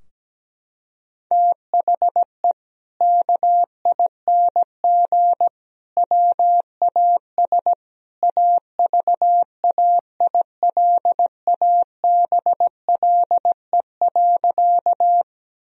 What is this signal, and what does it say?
Morse code, heard as a keyed tone, about 17 wpm, 700 Hz. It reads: THE KING WAS AVAILABLE.
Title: Sentences from Top 1000 Words - Review Part 3 - 17wpm